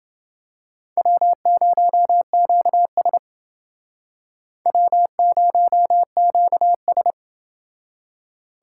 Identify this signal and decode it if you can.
Morse code, heard as a keyed tone, about 28 wpm, 700 Hz. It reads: W0QS W0QS